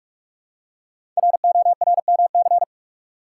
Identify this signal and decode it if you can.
Morse code, heard as a keyed tone, about 44 wpm, 690 Hz. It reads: RORMC
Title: Code Groups - 5 Letters - 45wpm